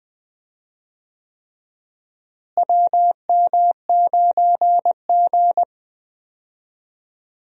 Morse code, heard as a keyed tone, about 20 wpm, 700 Hz. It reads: WM9G
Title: USA Call Signs - 4 Characters - Repeated 3x - 20wpm